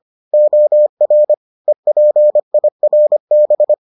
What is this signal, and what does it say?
Morse code, heard as a keyed tone, about 25 wpm, 600 Hz. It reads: OR EPIRB